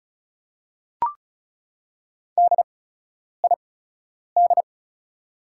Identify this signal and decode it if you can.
Morse code, heard as a keyed tone, about 36 wpm, spaced as for 10 wpm, 700 Hz. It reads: DID
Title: 8x Character Spacing - Top 200 Words - 35wpm